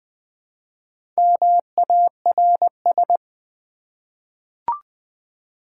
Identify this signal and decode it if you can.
Morse code, heard as a keyed tone, about 20 wpm, 700 Hz. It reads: MARS
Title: Easy Words with Hints - 20wpm